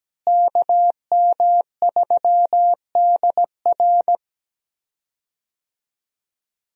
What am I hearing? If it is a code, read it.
Morse code, heard as a keyed tone, about 17 wpm, 700 Hz. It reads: KM3DR
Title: USA Call Signs - 5 Characters - Repeated 3x - 17wpm